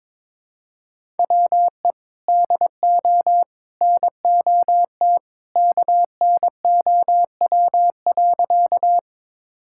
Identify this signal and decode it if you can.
Morse code, heard as a keyed tone, about 21 wpm, 700 Hz. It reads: WE DO NOT KNOW.
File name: practice / 22wpm / Sentences with 3-Word Phrases - 22wpm